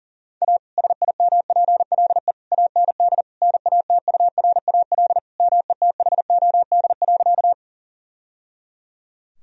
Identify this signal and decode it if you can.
Morse code, heard as a keyed tone, about 40 wpm, 700 Hz. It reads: A SIMPLE AND NATURAL METHOD.
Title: Sentences from Top 1000 Words - Review Part 4 - 40wpm